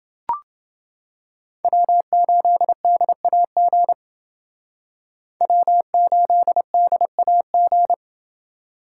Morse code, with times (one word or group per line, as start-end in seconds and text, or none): 1.6-3.9 W8DAG
5.4-7.9 W8DAG